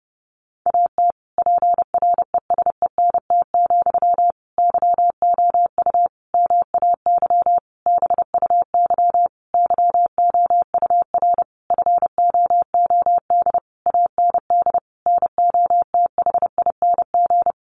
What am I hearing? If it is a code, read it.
Morse code, heard as a keyed tone, about 30 wpm, 700 Hz. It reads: AT PRESENT, YOU MAY BUY YOUR FOOD AND NOTHING
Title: Sentences from Top 800 Words 30wpm